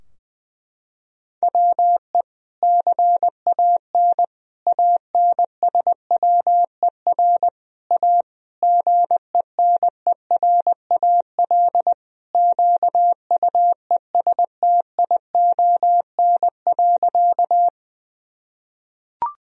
Morse code, heard as a keyed tone, about 19 wpm, 700 Hz. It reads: WE CAN ANSWER A GENERAL QUESTION.
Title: Sentences from Top 1000 Words - Review Part 3 - 20wpm